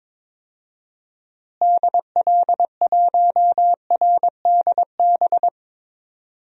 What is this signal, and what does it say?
Morse code, heard as a keyed tone, about 22 wpm, 700 Hz. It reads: DL1RDB
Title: DXCC-100 Call Signs - 6 Characters - Repeated 3x - 22wpm